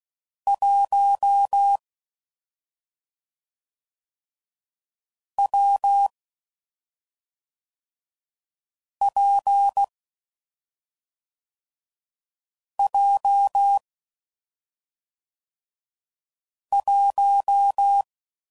Morse code, heard as a keyed tone, about 16 wpm, 800 Hz. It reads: 1 W P J 1
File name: Combinations / Difficult Combos / AWJ1